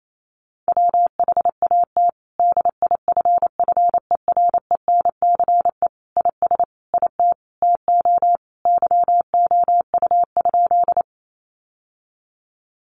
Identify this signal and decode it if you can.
Morse code, heard as a keyed tone, about 28 wpm, 700 Hz. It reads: WHAT DIFFERENCE IS IT TO YOU?
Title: Sentences from Top 800 Words 28wpm